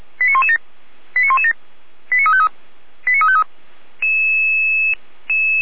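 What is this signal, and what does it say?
Five-tone selective call (ZVEI), heard as quick runs of five tones, tones of about 80 ms. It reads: 89117 89117 89353 89353
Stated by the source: ZVEI five-tone selective call received on an SDR.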